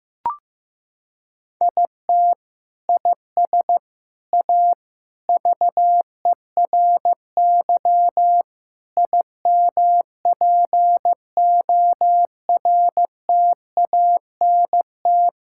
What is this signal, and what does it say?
Morse code, heard as a keyed tone, about 15 wpm, 700 Hz. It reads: IT IS A VERY IMPORTANT
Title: Sentences from Top 1600 Words 15wpm